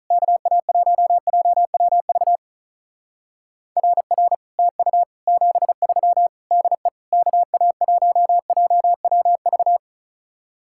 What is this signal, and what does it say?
Morse code, heard as a keyed tone, about 44 wpm, 700 Hz. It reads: KA1JWV RR TU 73 DE KA1JWV
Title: Speed-Racing Common QSO Phrases Encore - 35wpm